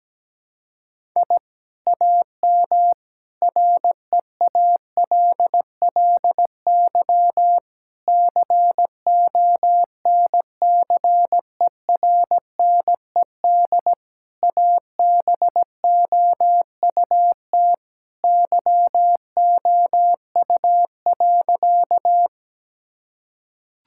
Morse code, heard as a keyed tone, about 17 wpm, 700 Hz. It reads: I AM REALLY CONCERNED ABOUT YOU.
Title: Sentences from Top 1000 Words - Review Part 6 - 17wpm